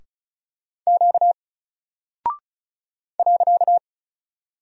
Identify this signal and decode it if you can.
Morse code, heard as a keyed tone, about 36 wpm, 700 Hz. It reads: Q .
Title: Single Character (Letter:Number:Punctuation:Prosign) 35wpm